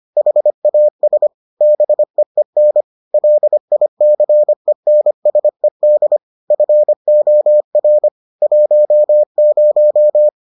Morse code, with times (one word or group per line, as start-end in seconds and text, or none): 0.2-1.3 HAS
1.6-2.8 BEEN
3.1-6.2 LICENSED
6.5-8.1 FOR
8.4-10.3 10